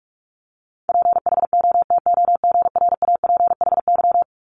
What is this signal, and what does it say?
Morse code, heard as a keyed tone, about 46 wpm, 700 Hz. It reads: PHOTOGRAPHY
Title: Multi-letter Phonemes 45wpm